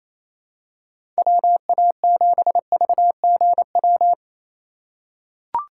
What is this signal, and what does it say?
Morse code, heard as a keyed tone, about 28 wpm, 700 Hz. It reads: WA7VGW
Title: USA Call Signs - 6 Characters - 4x Character Spacing - 28wpm